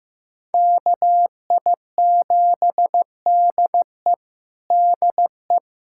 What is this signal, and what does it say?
Morse code, heard as a keyed tone, about 15 wpm, 700 Hz. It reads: KI7DE DE
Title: USA Callsign DE Callsign - 5 Characters - 15wpm